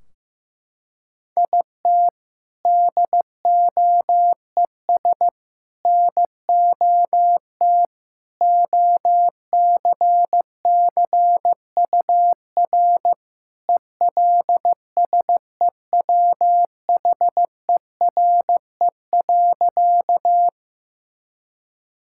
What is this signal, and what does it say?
Morse code, heard as a keyed tone, about 15 wpm, 700 Hz. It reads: IT DOES NOT OCCUR ELSEWHERE.
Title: Sentences from Top 1600 Words 15wpm